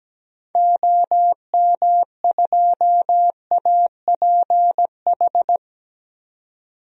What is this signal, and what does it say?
Morse code, heard as a keyed tone, about 17 wpm, 700 Hz. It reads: OM2APH
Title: DXCC-100 Call Signs - 6 Characters - 17wpm